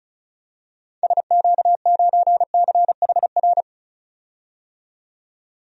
Morse code, heard as a keyed tone, about 36 wpm, 700 Hz. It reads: SQ9CHR
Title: DXCC-100 Call Signs - 6 Characters - 35wpm